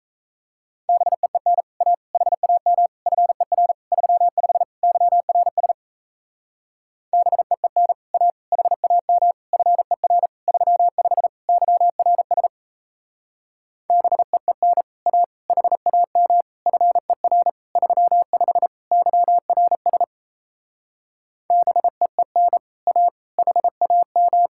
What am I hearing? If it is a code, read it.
Morse code, heard as a keyed tone, about 36 wpm, 700 Hz. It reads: BEEN A HAM FER 35 YRS BEEN A HAM FER 35 YRS BEEN A HAM FER 35 YRS BEEN A HAM